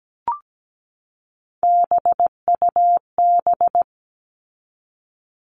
Morse code, heard as a keyed tone, about 17 wpm, 700 Hz. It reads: BUB
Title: All Cut Numbers - 3-Digit 17wpm